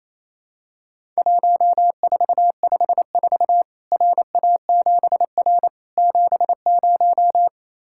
Morse code, heard as a keyed tone, about 28 wpm, 700 Hz. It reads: 1454 RA7R 70
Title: Worked All Europe DX Contest - QTC Exchange - 28wpm